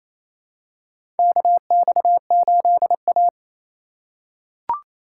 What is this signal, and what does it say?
Morse code, heard as a keyed tone, about 28 wpm, 700 Hz. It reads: KX8A
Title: USA Call Signs - 4 Characters - 2x Character Spacing - 28wpm